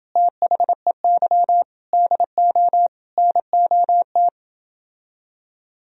Morse code, heard as a keyed tone, about 29 wpm, 700 Hz. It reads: THEY DO NOT